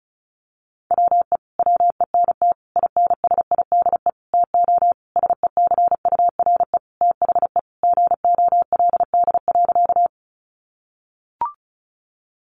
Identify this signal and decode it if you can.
Morse code, heard as a keyed tone, about 34 wpm, 700 Hz. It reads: WE WENT INSIDE TO SECURE THE GOLD.
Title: Sentences from Top 1200 Words 35wpm